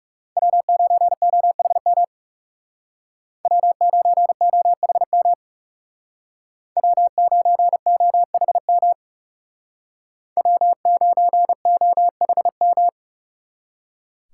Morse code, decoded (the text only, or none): W9OHM W9OHM W9OHM W9OHM